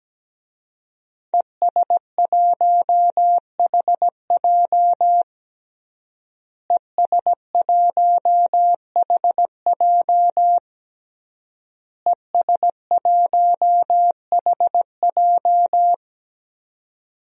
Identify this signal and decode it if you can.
Morse code, heard as a keyed tone, about 17 wpm, 700 Hz. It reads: ES1HJ ES1HJ ES1HJ
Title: DXCC-100 Call Signs - 5 Characters - Repeated 3x - 17wpm